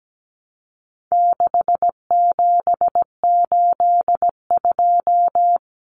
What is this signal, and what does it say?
Morse code, heard as a keyed tone, about 17 wpm, 700 Hz. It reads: 6782
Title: Common Cut Numbers - 4-Digit 17wpm